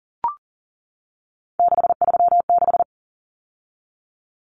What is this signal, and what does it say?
Morse code, heard as a keyed tone, about 40 wpm, 700 Hz. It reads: 636
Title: Numbers - 3-Digit 40wpm